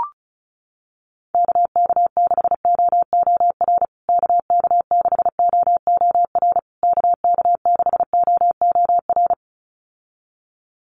Morse code, heard as a keyed tone, about 34 wpm, 700 Hz. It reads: KK6OOR KK6OOR KK6OOR
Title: USA Call Signs Repeated 3x 35wpm